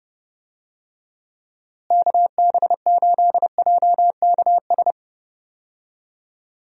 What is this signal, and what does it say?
Morse code, heard as a keyed tone, about 30 wpm, 700 Hz. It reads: KB8JKS
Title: USA Call Signs - 6 Characters - Repeated 2x - 30wpm